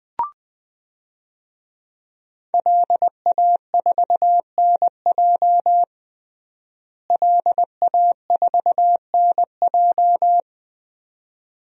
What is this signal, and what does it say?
Morse code, heard as a keyed tone, about 20 wpm, 700 Hz. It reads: LA4NJ LA4NJ
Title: DXCC-100 Call Signs - 5 Characters - Repeated 2x - 20wpm